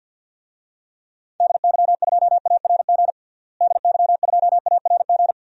Morse code, heard as a keyed tone, about 49 wpm, 700 Hz. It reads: DY2ARG DY2ARG
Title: DXCC-100 Call Signs - 6 Characters - Repeated 3x - 50wpm